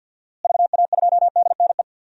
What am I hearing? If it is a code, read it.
Morse code, heard as a keyed tone, about 52 wpm, 700 Hz. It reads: VA2DNE
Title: DXCC-100 Call Signs - 6 Characters - Repeated 2x - 50wpm